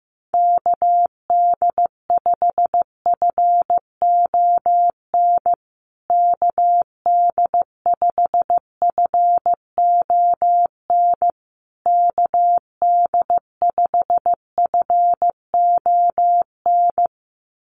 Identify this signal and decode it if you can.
Morse code, heard as a keyed tone, about 15 wpm, 700 Hz. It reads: KD5FON KD5FON KD5FON
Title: USA Call Signs Repeated 3x 15wpm